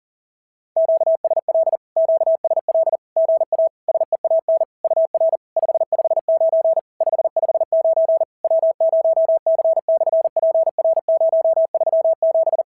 Morse code, heard as a keyed tone, about 40 wpm, 690 Hz. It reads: QSL QSL GA SEAN UR 559 559 W0C/PR037
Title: SOTA QSO Practice 40wpm